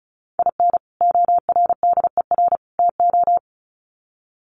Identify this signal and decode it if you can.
Morse code, heard as a keyed tone, about 34 wpm, 700 Hz. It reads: IN ORDER TO
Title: Sets of 3 Words 35wpm